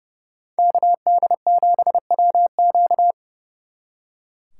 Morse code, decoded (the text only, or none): KD7WQ